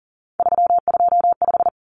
Morse code, heard as a keyed tone, about 40 wpm, 700 Hz. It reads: 325
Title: Numbers - 3-Digit 40wpm